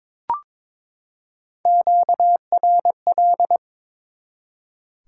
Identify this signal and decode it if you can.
Morse code, heard as a keyed tone, about 22 wpm, 700 Hz. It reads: QRL
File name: practice / 22wpm / Top 20 QSO Elements - 22wpm